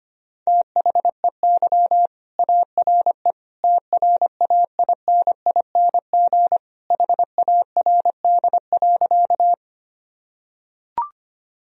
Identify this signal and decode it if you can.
Morse code, heard as a keyed tone, about 25 wpm, 700 Hz. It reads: THEY ARE TRAINING HARD.
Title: Sentences from Top 1000 Words - Review Part 4 - 25wpm